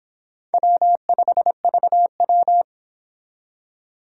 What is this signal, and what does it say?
Morse code, heard as a keyed tone, about 26 wpm, 700 Hz. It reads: W5VW